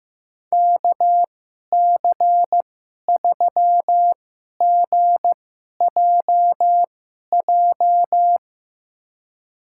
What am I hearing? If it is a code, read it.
Morse code, heard as a keyed tone, about 15 wpm, spaced as for 11 wpm, 700 Hz. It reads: KC3GJJ
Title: USA Call Signs - 6 Characters - 2x Character Spacing - 15wpm